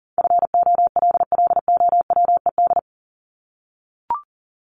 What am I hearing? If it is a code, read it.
Morse code, heard as a keyed tone, about 40 wpm, 700 Hz. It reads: FOLLOWED
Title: ED Suffix - Top 100 Words with Base Words 40wpm